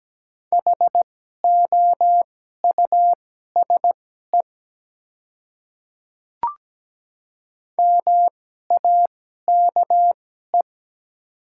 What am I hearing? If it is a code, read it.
Morse code, heard as a keyed tone, about 17 wpm, spaced as for 12 wpm, 700 Hz. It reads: HOUSE MAKE
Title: 2x Character Spacing - Top 300 Words - 17wpm